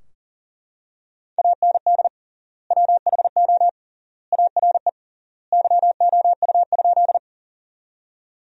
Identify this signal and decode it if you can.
Morse code, heard as a keyed tone, about 40 wpm, 700 Hz. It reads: AND WHO ARE YOU?
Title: 3x Word-Spacing - Sentences from Top 100 Words - 40wpm